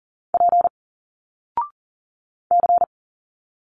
Morse code, heard as a keyed tone, about 40 wpm, 700 Hz. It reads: P C
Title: Single Letters 40wpm